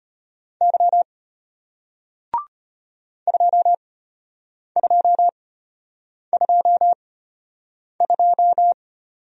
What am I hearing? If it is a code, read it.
Morse code, heard as a keyed tone, about 34 wpm, spaced as for 8 wpm, 700 Hz. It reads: Y 2222